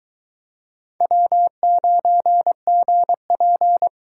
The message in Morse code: W9GP

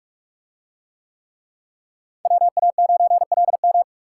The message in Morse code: WA9LM

